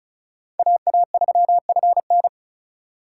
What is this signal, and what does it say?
Morse code, heard as a keyed tone, about 35 wpm, 690 Hz. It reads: AA3FN
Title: USA Call Signs - 5 Characters - 2x Character Spacing - 35wpm